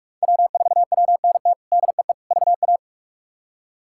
Morse code, heard as a keyed tone, about 46 wpm, 700 Hz. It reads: W4WNT DEE VA
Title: K1USN SST QSO Practice - 45wpm